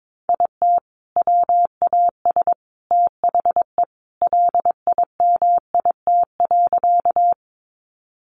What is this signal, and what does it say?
Morse code, heard as a keyed tone, about 22 wpm, 700 Hz. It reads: IT WAS THE LIMIT.
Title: Sentences from Top 1500 Words 22wpm